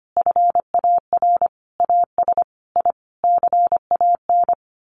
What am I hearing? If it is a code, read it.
Morse code, heard as a keyed tone, about 25 wpm, 700 Hz. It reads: FAR AS I CAN
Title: Sets of 5 Words Encore 25wpm